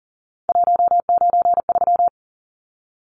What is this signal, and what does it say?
Morse code, heard as a keyed tone, about 40 wpm, 700 Hz. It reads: 193
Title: Numbers - 3-Digit 40wpm